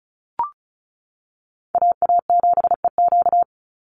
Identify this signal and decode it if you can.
Morse code, heard as a keyed tone, about 34 wpm, 700 Hz. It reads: AA7EQ
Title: NA SOTA Chaser Callsigns 35wpm